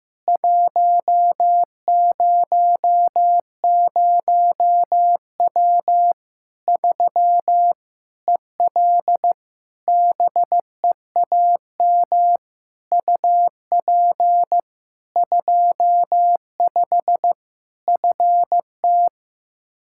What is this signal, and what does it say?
Morse code, heard as a keyed tone, about 15 wpm, 700 Hz. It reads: 100W 3 EL BEAM UP 25 FT
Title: Practice QSO - Exchange 2 - Info - 15wpm